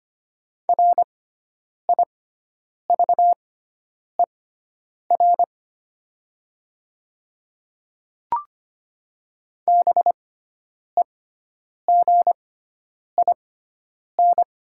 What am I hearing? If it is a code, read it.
Morse code, heard as a keyed tone, about 25 wpm, spaced as for 9 wpm, 700 Hz. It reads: RIVER BEGIN
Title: 6x Character Spacing - Top 300 Words - 25wpm